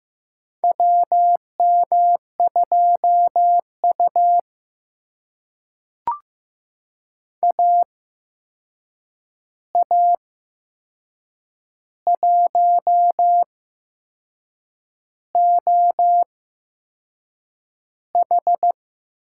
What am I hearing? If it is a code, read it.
Morse code, heard as a keyed tone, about 15 wpm, spaced as for 6 wpm, 700 Hz. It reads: WM2U AA1OH
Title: USA Call Signs - 8x Character Spacing - 15wpm